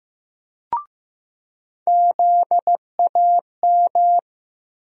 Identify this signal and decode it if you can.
Morse code, heard as a keyed tone, about 15 wpm, 700 Hz. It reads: ZAM
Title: USA Call Sign Postfixes - 3 Letters 15wpm